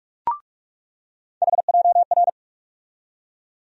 Morse code, heard as a keyed tone, about 46 wpm, 700 Hz. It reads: HJR
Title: USA Call Sign Postfixes - 3 Letters 45wpm